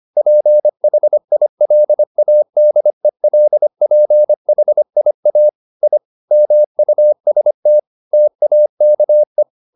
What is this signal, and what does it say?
Morse code, heard as a keyed tone, about 25 wpm, 600 Hz. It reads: PHILADELPHIA I MUST TAKE
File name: the old man and the sea part 1 25wpm